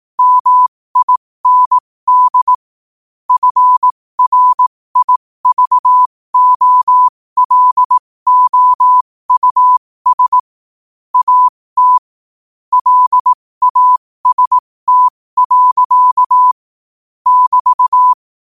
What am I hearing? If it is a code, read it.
Morse code, heard as a keyed tone, about 18 wpm, spaced as for 15 wpm, 1000 Hz. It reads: MIND FRIVOLOUS AT LAST. <BT>
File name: qotdfiles / 15wpm